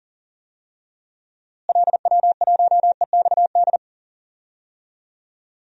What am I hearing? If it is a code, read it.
Morse code, heard as a keyed tone, about 40 wpm, 700 Hz. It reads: LW1EXD
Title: DXCC-100 Call Signs - 6 Characters - Repeated 2x - 40wpm